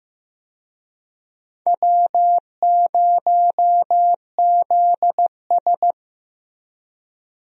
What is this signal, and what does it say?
Morse code, heard as a keyed tone, about 15 wpm, 700 Hz. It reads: W0ZS
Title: USA Call Signs - 4 Characters - Repeated 3x - 15wpm